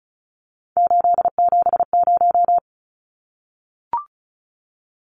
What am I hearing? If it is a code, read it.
Morse code, heard as a keyed tone, about 34 wpm, 700 Hz. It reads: 870
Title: Numbers - 3-Digit 35wpm